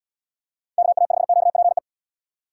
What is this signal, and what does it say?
Morse code, heard as a keyed tone, about 76 wpm, 700 Hz. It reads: KA5QPE